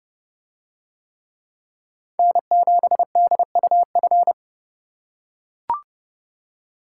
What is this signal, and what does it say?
Morse code, heard as a keyed tone, about 29 wpm, 700 Hz. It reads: N7DUF